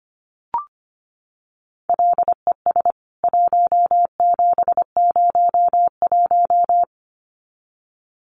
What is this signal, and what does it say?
Morse code, heard as a keyed tone, about 25 wpm, 700 Hz. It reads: LES 1701
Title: CWT Exchange 25wpm